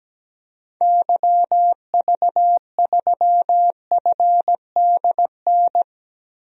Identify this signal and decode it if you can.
Morse code, heard as a keyed tone, about 17 wpm, 700 Hz. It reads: YV3FDN
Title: DXCC-100 Call Signs - 6 Characters - Repeated 3x - 17wpm